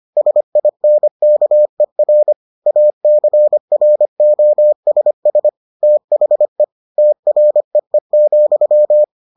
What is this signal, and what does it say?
Morse code, heard as a keyed tone, about 25 wpm, 600 Hz. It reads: SINKER ACROSS THE TREE,